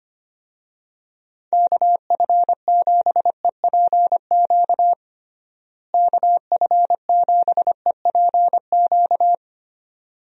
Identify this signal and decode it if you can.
Morse code, heard as a keyed tone, about 25 wpm, 700 Hz. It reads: KF7EPQ KF7EPQ
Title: USA Call Signs - 6 Characters - Repeated 2x - 25wpm